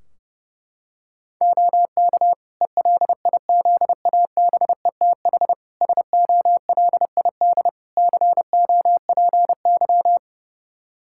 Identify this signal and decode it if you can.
Morse code, heard as a keyed tone, about 29 wpm, 700 Hz. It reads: OK ELIZABETH SOLID COPY
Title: Practice QSO - Exchange 3 - More Info - 30wpm